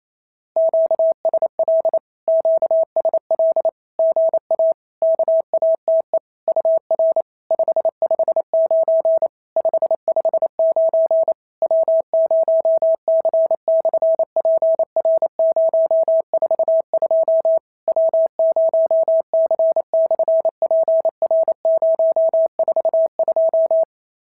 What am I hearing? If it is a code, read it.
Morse code, heard as a keyed tone, about 28 wpm, 650 Hz. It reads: QSL QSL GA KATE UR 559 559 W0C/PR042 W0C/PR042
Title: SOTA QSO Practice 28wpm